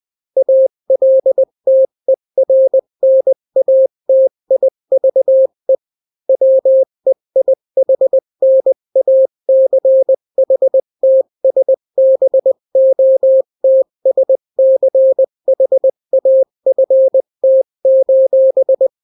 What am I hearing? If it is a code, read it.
Morse code, heard as a keyed tone, about 20 wpm, 530 Hz. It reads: ALTERNATIVE WEIHNACHTSBOTSCHAFT:
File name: daily / 18wpm